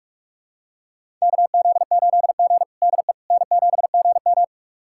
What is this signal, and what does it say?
Morse code, heard as a keyed tone, about 44 wpm, 690 Hz. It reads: KZ8G DE N7GM
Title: USA Callsign DE Callsign - 4 Characters - 45wpm